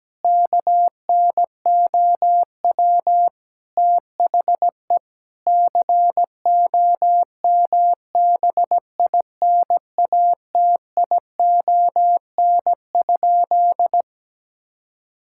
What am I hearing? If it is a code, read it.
Morse code, heard as a keyed tone, about 17 wpm, 700 Hz. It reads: KNOW THE COMBINATION?
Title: Sentences from Top 1000 Words - Review Part 6 - 17wpm